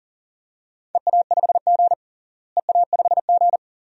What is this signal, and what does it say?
Morse code, heard as a keyed tone, about 40 wpm, 700 Hz. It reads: EA5G EA5G
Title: DXCC-100 Call Signs - 4 Characters - Repeated 2x - 40wpm